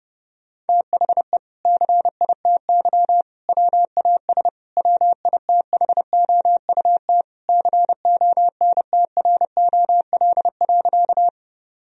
Morse code, heard as a keyed tone, about 30 wpm, 700 Hz. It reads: THE CITY WAS WITHOUT CONTROL.
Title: Sentences from Top 1000 Words - Review Part 3 - 30wpm